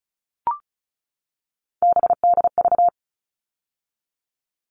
Morse code, heard as a keyed tone, about 34 wpm, 700 Hz. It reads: BDV
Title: All Cut Numbers - 3-Digit 35wpm